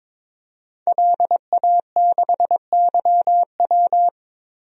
Morse code, heard as a keyed tone, about 22 wpm, 700 Hz. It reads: LA6YW
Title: DXCC-100 Call Signs - 5 Characters - 22wpm